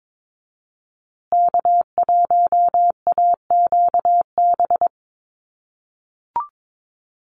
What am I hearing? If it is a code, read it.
Morse code, heard as a keyed tone, about 22 wpm, 700 Hz. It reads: K1AQB
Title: USA Call Signs Repeated 3x 22wpm